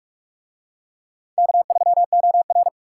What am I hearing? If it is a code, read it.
Morse code, heard as a keyed tone, about 44 wpm, 700 Hz. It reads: K3OR